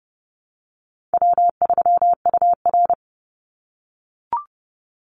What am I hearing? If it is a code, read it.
Morse code, heard as a keyed tone, about 31 wpm, 700 Hz. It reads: W3UR
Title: Easy Call Signs - 30wpm